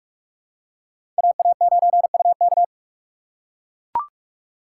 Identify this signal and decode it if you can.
Morse code, heard as a keyed tone, about 42 wpm, 700 Hz. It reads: AA9UK
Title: USA Call Signs - 5 Characters - 4x Character Spacing - 45wpm